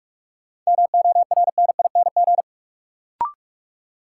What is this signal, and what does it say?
Morse code, heard as a keyed tone, about 42 wpm, 700 Hz. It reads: MORNING
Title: Sentences from Top 400 Words - New Word Drill - 45wpm